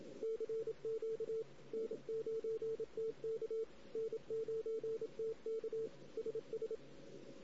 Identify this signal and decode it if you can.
Morse code, heard as a keyed tone, about 27 wpm, 450 Hz. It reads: CQ N9TK N9TK SS
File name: CW / 2008-ss-cw-k5tr-op-n5rz -16:00